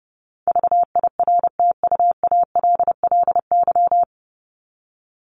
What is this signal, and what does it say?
Morse code, heard as a keyed tone, about 30 wpm, 700 Hz. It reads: VIRTUALLY
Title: Top 1501-1750 Words 30wpm